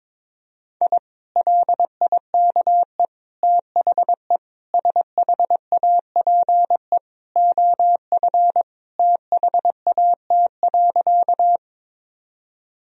Morse code, heard as a keyed tone, about 22 wpm, 700 Hz. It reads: I LIKE THE SHAPE OF THAT.
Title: Sentences from Top 1000 Words - Review Part 3 - 22wpm